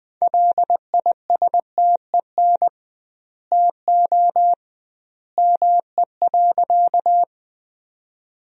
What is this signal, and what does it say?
Morse code, heard as a keyed tone, about 21 wpm, 700 Hz. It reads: LISTEN TO ME.
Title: 2x Word-Spacing - Sentences from Top 400 Words - 20wpm